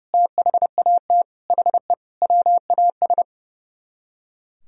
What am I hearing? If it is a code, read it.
Morse code, heard as a keyed tone, about 37 wpm, 700 Hz. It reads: THAT HE WAS
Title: Speed-Racing - Sets of 5 Words 30wpm